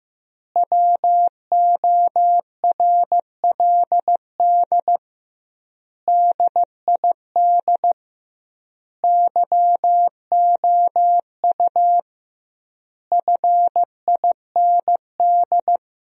Morse code, heard as a keyed tone, about 16 wpm, 700 Hz. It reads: WORLD DID YOU FIND
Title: 2x Word-Spacing - Sentences from Top 200 Words - 15wpm